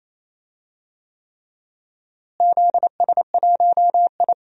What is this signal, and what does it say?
Morse code, heard as a keyed tone, about 28 wpm, 700 Hz. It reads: ZS1I
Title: DXCC-100 Call Signs - 4 Characters - Repeated 3x - 28wpm